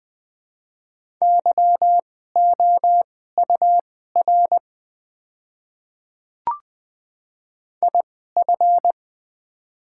Morse code, heard as a keyed tone, about 20 wpm, spaced as for 14 wpm, 700 Hz. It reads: YOUR IF
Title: 2x Character Spacing - Top 100 Words - 20wpm